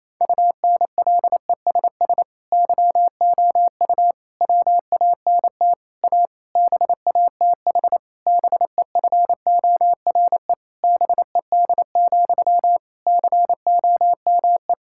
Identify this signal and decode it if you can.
Morse code, heard as a keyed tone, about 28 wpm, 700 Hz. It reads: UNLESS YOU WANT A BATH BEFORE BED, COME
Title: Sentences from Top 1600 Words 28wpm